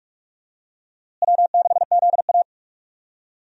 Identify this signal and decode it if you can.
Morse code, heard as a keyed tone, about 46 wpm, 700 Hz. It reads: W6ZA
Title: USA Call Signs - 4 Characters - Repeated 2x - 45wpm